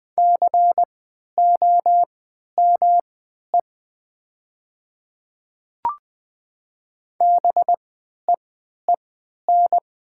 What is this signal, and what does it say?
Morse code, heard as a keyed tone, about 20 wpm, spaced as for 11 wpm, 700 Hz. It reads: COME BEEN
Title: 3x Character Spacing - Top 100 Words - 20wpm